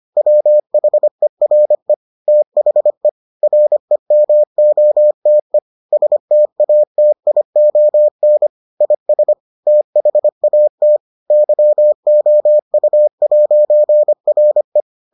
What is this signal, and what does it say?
Morse code, heard as a keyed tone, about 25 wpm, 600 Hz. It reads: WHERE THE REMOTE STATION IS THAT YOU'RE